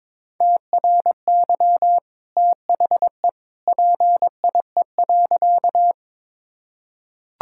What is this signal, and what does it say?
Morse code, heard as a keyed tone, about 22 wpm, 700 Hz. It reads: TRY THE PIE.
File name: practice / 22wpm / Easy Sentences with Hints - 22wpm